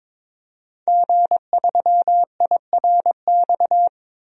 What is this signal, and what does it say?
Morse code, heard as a keyed tone, about 22 wpm, 700 Hz. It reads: G3IRX
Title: DXCC-100 Call Signs - 5 Characters - Repeated 3x - 22wpm